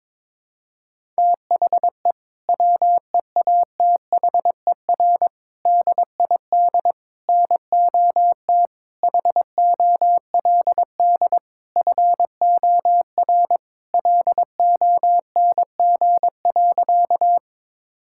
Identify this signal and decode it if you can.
Morse code, heard as a keyed tone, about 22 wpm, 700 Hz. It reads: THE WEATHER DID NOT HOLD FOR LONG.
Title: Sentences from Top 1000 Words - Review Part 5 - 22wpm